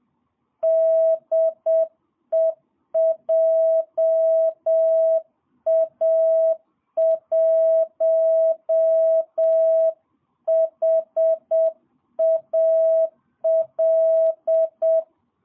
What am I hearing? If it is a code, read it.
Morse code, heard as a keyed tone, about 7 wpm, 650 Hz. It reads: DE JA1HAL
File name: jh8cbh / 2022cw